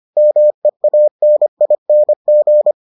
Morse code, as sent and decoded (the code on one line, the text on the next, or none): -- . .- -. .. -. --.
MEANING